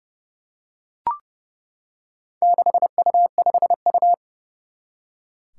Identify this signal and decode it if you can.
Morse code, heard as a keyed tone, about 30 wpm, 700 Hz. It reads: 6U5U
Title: Four Challenging Letter-Number Combinations 30wpm